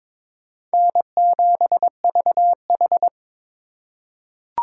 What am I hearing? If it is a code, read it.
Morse code, heard as a keyed tone, about 21 wpm, 700 Hz. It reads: N7VH
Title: USA Call Signs - 4 Characters - 3x Character Spacing - 22wpm